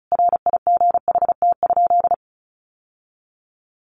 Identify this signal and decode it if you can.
Morse code, heard as a keyed tone, about 34 wpm, 700 Hz. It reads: RIGHT?
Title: Sentences from Top 700 Words 35wpm